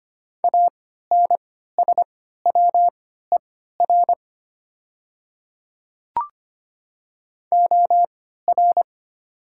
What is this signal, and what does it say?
Morse code, heard as a keyed tone, about 25 wpm, spaced as for 14 wpm, 700 Hz. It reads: ANSWER OR
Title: 3x Character Spacing - Top 300 Words - 25wpm